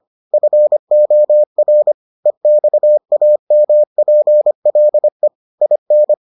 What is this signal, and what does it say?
Morse code, heard as a keyed tone, about 25 wpm, 600 Hz. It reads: FOR EXAMPLE IN